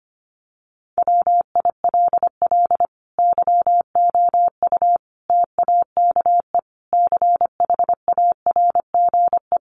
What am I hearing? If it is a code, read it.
Morse code, heard as a keyed tone, about 25 wpm, 700 Hz. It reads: WILL YOU TAKE CHARGE
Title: Sentences from Top 900 Words 25wpm